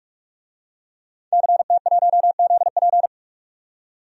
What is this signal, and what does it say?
Morse code, heard as a keyed tone, about 44 wpm, 690 Hz. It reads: CT1ZP
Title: DXCC-100 Call Signs - 5 Characters - 45wpm